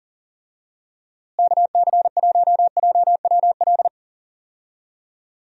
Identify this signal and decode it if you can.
Morse code, heard as a keyed tone, about 38 wpm, 700 Hz. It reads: KC1JWL
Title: USA Call Signs - 6 Characters - Repeated 2x - 40wpm